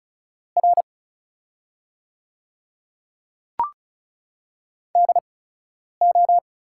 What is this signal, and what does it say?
Morse code, heard as a keyed tone, about 36 wpm, spaced as for 10 wpm, 690 Hz. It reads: R DO
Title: 8x Character Spacing - Top 200 Words - 35wpm